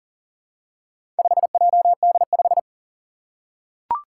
Morse code, heard as a keyed tone, about 40 wpm, 700 Hz. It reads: 5JD5
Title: Four Challenging Letter-Number Combinations 40wpm